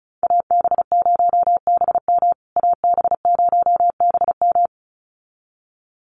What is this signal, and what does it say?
Morse code, heard as a keyed tone, about 34 wpm, 700 Hz. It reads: AB0BM AB0BM
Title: USA Call Signs Repeated 2x 35wpm